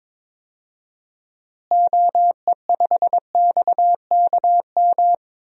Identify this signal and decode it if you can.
Morse code, heard as a keyed tone, about 21 wpm, 700 Hz. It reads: OE5XKM
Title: DXCC-100 Call Signs - 6 Characters - Repeated 3x - 22wpm